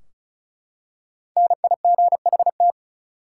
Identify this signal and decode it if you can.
Morse code, heard as a keyed tone, about 36 wpm, 700 Hz. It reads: NIGHT
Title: Five Letter Words - 35wpm